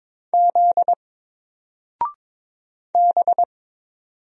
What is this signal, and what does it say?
Morse code, heard as a keyed tone, about 22 wpm, 700 Hz. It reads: Z B
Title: Single Letter-Number 22wpm